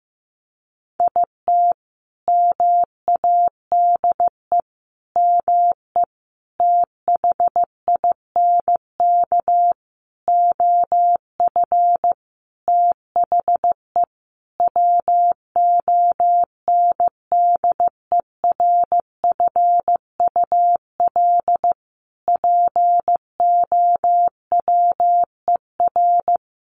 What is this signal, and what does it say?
Morse code, heard as a keyed tone, about 15 wpm, 700 Hz. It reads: IT MADE ME THINK OF THE WONDERFUL POWER
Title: Sentences from Top 1100 Words 15wpm